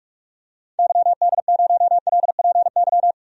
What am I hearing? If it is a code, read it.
Morse code, heard as a keyed tone, about 46 wpm, 700 Hz. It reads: YD0LPY
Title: DXCC-100 Call Signs - 6 Characters - 45wpm